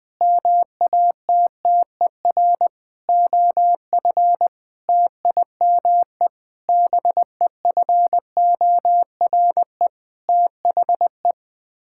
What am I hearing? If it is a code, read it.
Morse code, heard as a keyed tone, about 21 wpm, 700 Hz. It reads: MATTER OF TIME BEFORE THE
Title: Sets of 9 Words - 20wpm